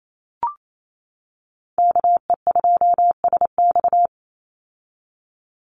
Morse code, heard as a keyed tone, about 28 wpm, 700 Hz. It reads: KE2SX
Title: Call Signs - 28wpm